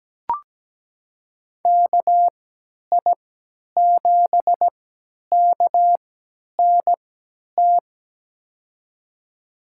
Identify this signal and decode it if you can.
Morse code, heard as a keyed tone, about 17 wpm, spaced as for 10 wpm, 700 Hz. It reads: KI7KNT
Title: USA Call Signs - 6 Characters - 3x Character Spacing - 17wpm